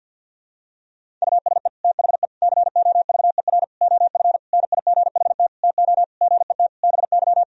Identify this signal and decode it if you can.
Morse code, heard as a keyed tone, about 49 wpm, 700 Hz. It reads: USE THE COVER OF NIGHT TO GET BY
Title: Sentences from Top 1000 Words - Review Part 1 - 50wpm